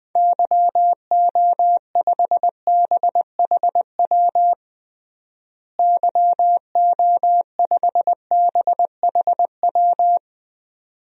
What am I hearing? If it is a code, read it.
Morse code, heard as a keyed tone, about 20 wpm, 700 Hz. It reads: YO5BHW YO5BHW
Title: DXCC-100 Call Signs - 6 Characters - Repeated 2x - 20wpm